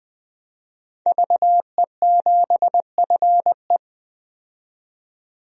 Morse code, heard as a keyed tone, about 20 wpm, 700 Hz. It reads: VE7FE
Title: DXCC-100 Call Signs - 5 Characters - 20wpm